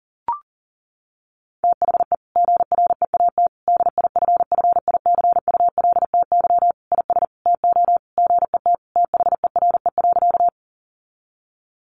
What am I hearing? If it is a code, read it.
Morse code, heard as a keyed tone, about 40 wpm, 700 Hz. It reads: THE GREAT DIFFICULTY IS TO GET THERE.
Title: Sentences from Top 1500 Words 40wpm